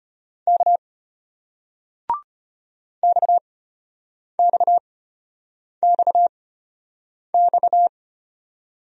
Morse code, heard as a keyed tone, about 34 wpm, spaced as for 8 wpm, 690 Hz. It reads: K XXXX